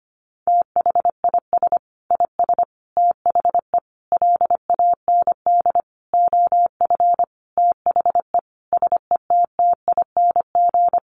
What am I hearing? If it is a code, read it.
Morse code, heard as a keyed tone, about 25 wpm, 700 Hz. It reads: THIS IS THE LAND OF THE SETTING
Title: Sentences from Top 1300 Words 25wpm